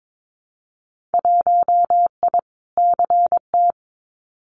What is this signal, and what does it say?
Morse code, heard as a keyed tone, about 22 wpm, 700 Hz. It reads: 1I CT
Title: Winter Field Day Exchanges 22wpm